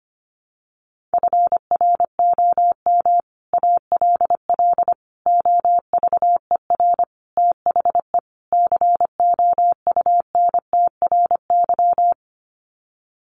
Morse code, heard as a keyed tone, about 25 wpm, 700 Hz. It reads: FROM ALL OVER THE COUNTRY
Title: Sets of 5 Words Encore 25wpm